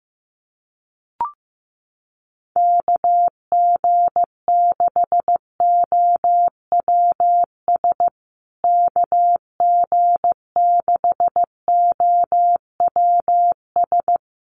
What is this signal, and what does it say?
Morse code, heard as a keyed tone, about 15 wpm, 700 Hz. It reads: KG6OWS KG6OWS
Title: USA Call Signs Repeated 3x 15wpm